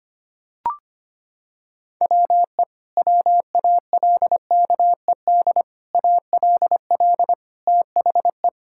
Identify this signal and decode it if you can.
Morse code, heard as a keyed tone, about 25 wpm, 700 Hz. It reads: WE WALKED ALL THE
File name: practice / 25wpm / Sentences with 4-Word Phrases - 25wpm